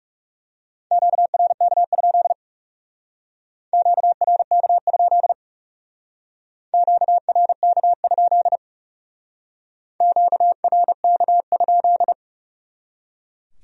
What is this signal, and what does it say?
Morse code, heard as a keyed tone, about 38 wpm, 700 Hz. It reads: QRK? QRK? QRK? QRK?